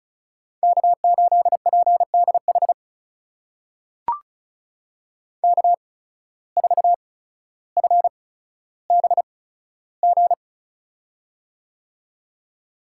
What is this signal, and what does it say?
Morse code, heard as a keyed tone, about 36 wpm, spaced as for 15 wpm, 700 Hz. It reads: K8PDH K4FBG
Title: USA Call Signs - 5 Characters - 8x Character Spacing - 35wpm